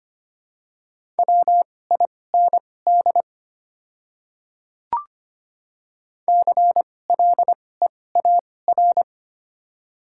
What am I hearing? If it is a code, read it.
Morse code, heard as a keyed tone, about 25 wpm, spaced as for 18 wpm, 700 Hz. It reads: WIND CLEAR
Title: 2x Character Spacing - Top 500 Words - 25wpm